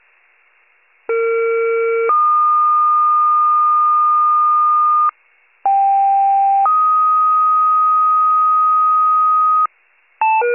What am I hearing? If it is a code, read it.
Two-tone page, heard as a one-second tone, then a three-second tone, about 460 then 1220 Hz, about 790 then 1250 Hz. It is heard from 1.1 to 9.7 s.